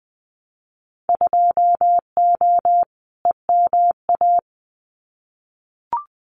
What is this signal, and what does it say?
Morse code, heard as a keyed tone, about 20 wpm, 700 Hz. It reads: 2O EMA
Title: Winter Field Day Exchanges 20wpm